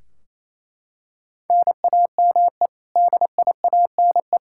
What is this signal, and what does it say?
Morse code, heard as a keyed tone, about 27 wpm, 700 Hz. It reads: NAME DIANE
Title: K1USN SST QSO Practice 28wpm